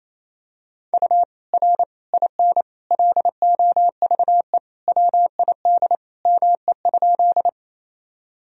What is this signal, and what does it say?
Morse code, heard as a keyed tone, about 28 wpm, 700 Hz. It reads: U R IN LOVE WID ME?
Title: Sentences from Top 400 Words and Abbreviations 28wpm